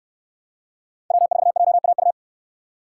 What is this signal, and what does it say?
Morse code, heard as a keyed tone, about 69 wpm, 700 Hz. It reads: W4JAU